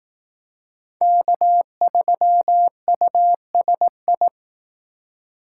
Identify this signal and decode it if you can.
Morse code, heard as a keyed tone, about 18 wpm, 700 Hz. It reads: K3USI
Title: Speed Racing - USA 1x3 Call Signs 15wpm